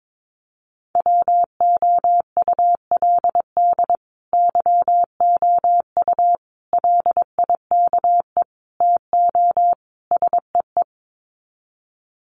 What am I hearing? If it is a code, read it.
Morse code, heard as a keyed tone, about 22 wpm, 700 Hz. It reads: WOULD YOU LIKE TO SEE
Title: Sets of 5 Words Encore2 22wpm